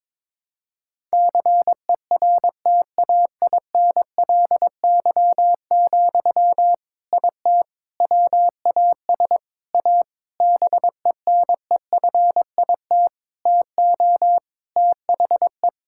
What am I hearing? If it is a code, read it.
Morse code, heard as a keyed tone, about 22 wpm, 700 Hz. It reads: CERTAINLY, IT WAS A BENEFIT TO THE
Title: Sentences from Top 1000 Words - Review Part 6 - 22wpm